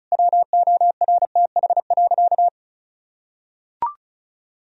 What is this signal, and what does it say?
Morse code, heard as a keyed tone, about 36 wpm, 700 Hz. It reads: WORTH.
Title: Sentences from Top 1000 Words - Review Part 4 - 35wpm